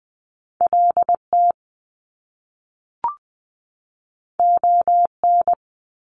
Morse code, heard as a keyed tone, about 20 wpm, 700 Hz. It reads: LT ON
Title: Top Two Letter Combinations in Words 20wpm